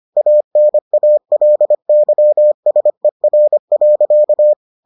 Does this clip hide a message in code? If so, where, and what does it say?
Yes: Morse code from 0.2 to 4.5 s, heard as a keyed tone, about 25 wpm, 600 Hz. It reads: ANALYSER.